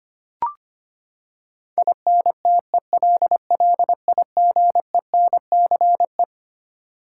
Morse code, heard as a keyed tone, about 25 wpm, 700 Hz. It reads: INTELLIGENCE
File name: Twelve Letter Words - 25wpm